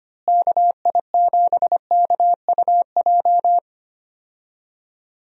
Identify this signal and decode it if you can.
Morse code, heard as a keyed tone, about 25 wpm, 700 Hz. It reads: KI7KUJ
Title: USA Call Signs - 6 Characters - Repeated 3x - 25wpm